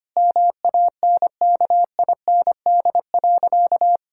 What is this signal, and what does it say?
Morse code, heard as a keyed tone, about 25 wpm, 700 Hz. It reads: MANKIND.
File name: practice / 25wpm / Difficult Sentences with Hints - 25wpm